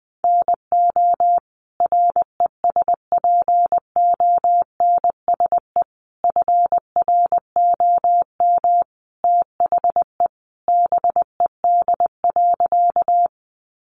Morse code, heard as a keyed tone, about 20 wpm, 700 Hz. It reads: NO RESPONSE FROM THE BED.
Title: Sentences from Top 1400 Words 20wpm